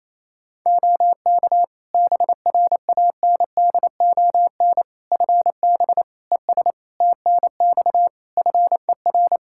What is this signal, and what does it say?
Morse code, heard as a keyed tone, about 28 wpm, 700 Hz. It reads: OK BRANDON FB ES TNX FER
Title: K1USN SST QSO Practice 28wpm